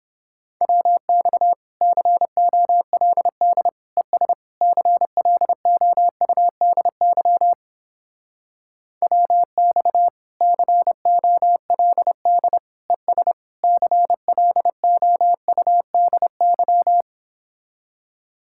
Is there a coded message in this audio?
Yes: Morse code from 0.6 to 17.0 s, heard as a keyed tone, about 29 wpm, 700 Hz. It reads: WX COLD ES CLOUDY WX COLD ES CLOUDY